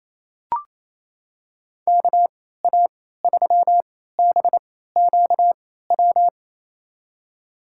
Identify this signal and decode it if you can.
Morse code, heard as a keyed tone, about 28 wpm, spaced as for 16 wpm, 700 Hz. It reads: KA3BQW
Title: USA Call Signs - 6 Characters - 3x Character Spacing - 28wpm